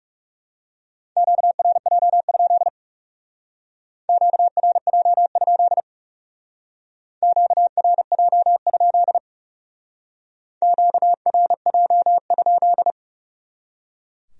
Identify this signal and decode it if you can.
Morse code, heard as a keyed tone, about 37 wpm, 700 Hz. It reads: QRJ? QRJ? QRJ? QRJ?
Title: Speed-Racing Q-Codes - 30wpm